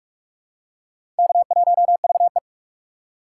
Morse code, heard as a keyed tone, about 44 wpm, 690 Hz. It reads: K1VE
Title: USA Call Signs - 4 Characters - 2x Character Spacing - 45wpm